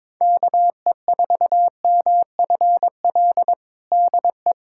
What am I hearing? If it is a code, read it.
Morse code, heard as a keyed tone, about 22 wpm, 700 Hz. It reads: KE4MFL DE
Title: USA Callsign DE Callsign 22wpm